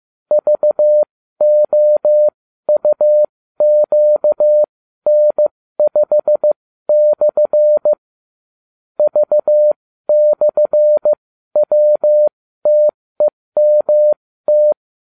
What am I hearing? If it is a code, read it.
Morse code, heard as a keyed tone, about 15 wpm, spaced as for 13 wpm, 600 Hz. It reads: VOUQN5/ V/WTEMT